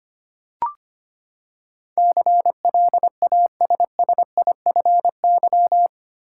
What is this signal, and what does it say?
Morse code, heard as a keyed tone, about 25 wpm, 700 Hz. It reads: CLASSIFY
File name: Eight Letter Words - 25wpm